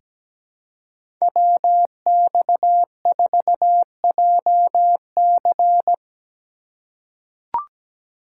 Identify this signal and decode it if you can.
Morse code, heard as a keyed tone, about 17 wpm, 700 Hz. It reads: WX4JC
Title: USA Call Signs - 5 Characters - 4x Character Spacing - 17wpm